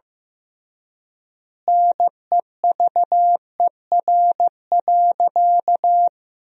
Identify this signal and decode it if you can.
Morse code, heard as a keyed tone, about 15 wpm, 700 Hz. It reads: NEVER.
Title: 3x Word-Spacing - Sentences from Top 300 Words - 15wpm